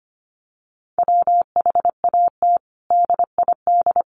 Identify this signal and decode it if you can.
Morse code, heard as a keyed tone, about 25 wpm, 700 Hz. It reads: WHAT DID
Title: Sentences from Top 1100 Words 25wpm